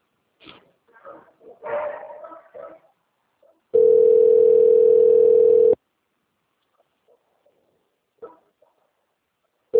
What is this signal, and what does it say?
Ringback tone (North America), 440 + 480 Hz, with speech.
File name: rg-1600-unknown- - - .563